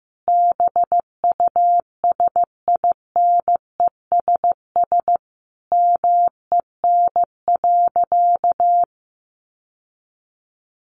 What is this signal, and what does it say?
Morse code, heard as a keyed tone, about 15 wpm, 700 Hz. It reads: BUSINESS MEN.
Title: Sentences from Top 1200 Words 15wpm